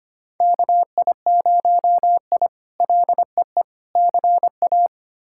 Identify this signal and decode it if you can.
Morse code, heard as a keyed tone, about 25 wpm, 700 Hz. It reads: KI0I LEE CA
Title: K1USN SST QSO Practice - 25wpm